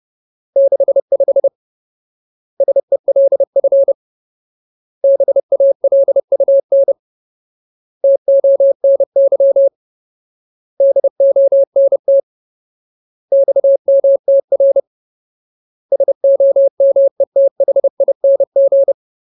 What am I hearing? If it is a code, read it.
Morse code, heard as a keyed tone, about 30 wpm, 550 Hz. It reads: 65 SELF BALUN TONY DONT XMTR SOMETHING
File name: Ultimate Word List / 30WPM Ultimate